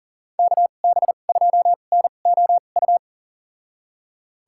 Morse code, heard as a keyed tone, about 40 wpm, spaced as for 29 wpm, 700 Hz. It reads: KB2NOU
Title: USA Call Signs - 6 Characters - 2x Character Spacing - 40wpm